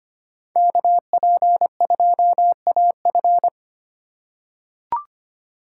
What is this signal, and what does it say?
Morse code, heard as a keyed tone, about 25 wpm, 700 Hz. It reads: KP2AF
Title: USA Call Signs - 5 Characters - 2x Character Spacing - 25wpm